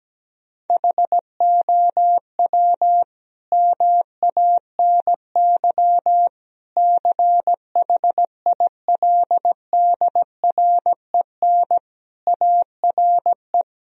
Morse code, heard as a keyed tone, about 17 wpm, 700 Hz. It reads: HOW MANY CHILDREN ARE
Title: Sentences from Top 1000 Words - Review Part 4 - 17wpm